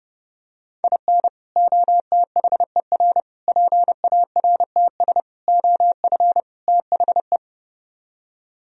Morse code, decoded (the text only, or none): IN OTHER PARTS OF THE